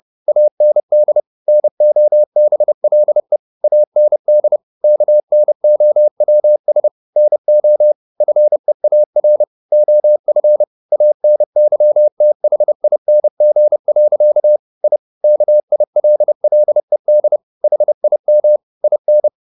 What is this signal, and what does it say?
Morse code, heard as a keyed tone, about 30 wpm, 600 Hz. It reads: AND NOBLE AND KNOWS NO FEAR OF ANYTHING. I KILLED HIM IN